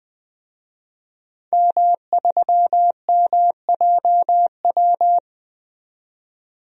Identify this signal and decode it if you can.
Morse code, heard as a keyed tone, about 20 wpm, 700 Hz. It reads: M3MJW
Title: DXCC-100 Call Signs - 5 Characters - 20wpm